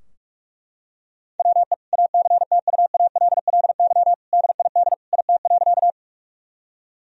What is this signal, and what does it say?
Morse code, heard as a keyed tone, about 41 wpm, 700 Hz. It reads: WE ACTUALLY DID IT.
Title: Sentences from Top 1000 Words - Review Part 4 - 45wpm